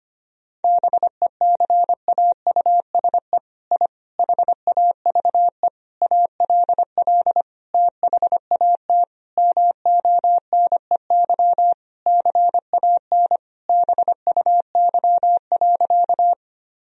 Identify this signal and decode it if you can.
Morse code, heard as a keyed tone, about 25 wpm, 700 Hz. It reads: BECAUSE I HAVE ALL THAT MONEY CAN BUY.
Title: Sentences from Top 1000 Words - Review Part 4 - 25wpm